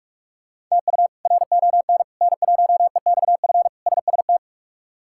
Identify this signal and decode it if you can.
Morse code, heard as a keyed tone, about 48 wpm, 700 Hz. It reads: TU RON N1EXF SST
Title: K1USN SST QSO Practice - 45wpm